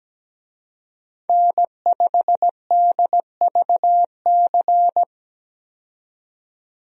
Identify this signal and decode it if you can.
Morse code, heard as a keyed tone, about 17 wpm, 700 Hz. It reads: N5DVC
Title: USA Call Signs - 5 Characters - 17wpm